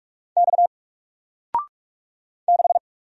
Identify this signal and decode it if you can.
Morse code, heard as a keyed tone, about 43 wpm, 700 Hz. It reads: X 6
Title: Single Character (Letter:Number:Punctuation:Prosign) Alt 45wpm